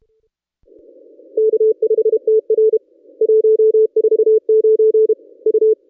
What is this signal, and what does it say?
Morse code, heard as a keyed tone, about 28 wpm, 440 Hz. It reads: K5TR 149 U
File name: contests / ss-cw-k5tr-op-n5rz -1000